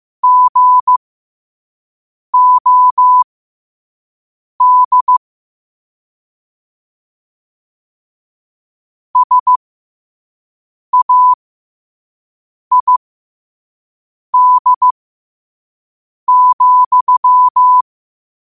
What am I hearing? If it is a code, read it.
Morse code, heard as a keyed tone, about 15 wpm, spaced as for 5 wpm, 1000 Hz. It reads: GOD SAID,
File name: qotdfiles / 5wpm